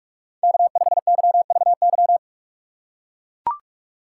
Morse code, heard as a keyed tone, about 44 wpm, 700 Hz. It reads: K5YVY
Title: K1USN SST QSO Practice - 45wpm